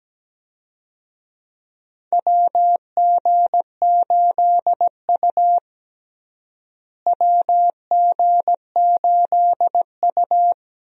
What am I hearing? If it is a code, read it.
Morse code, heard as a keyed tone, about 17 wpm, 700 Hz. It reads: WG8U WG8U
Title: USA Call Signs - 4 Characters - Repeated 2x - 17wpm